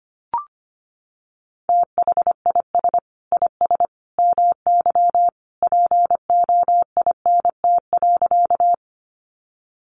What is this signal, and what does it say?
Morse code, heard as a keyed tone, about 25 wpm, 700 Hz. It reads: THIS IS MY POINT.
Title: Sentences from Top 200 Words 25wpm